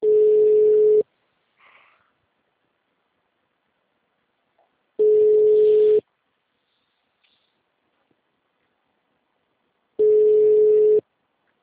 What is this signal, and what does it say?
Ringback tone (Europe, 425 Hz).